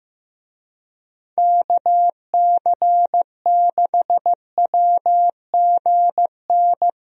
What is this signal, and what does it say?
Morse code, heard as a keyed tone, about 16 wpm, 700 Hz. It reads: KC6WGN
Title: USA Call Signs - 6 Characters - Repeated 2x - 15wpm